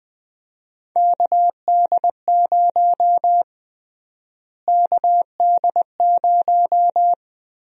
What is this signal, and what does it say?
Morse code, heard as a keyed tone, about 20 wpm, 700 Hz. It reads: KD0 KD0
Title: USA Call Sign Prefixes - Repeated 2x - 20wpm